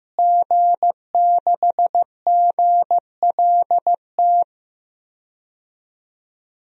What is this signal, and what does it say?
Morse code, heard as a keyed tone, about 15 wpm, 700 Hz. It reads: G6GLT
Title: DXCC-100 Call Signs - 5 Characters - Repeated 2x - 15wpm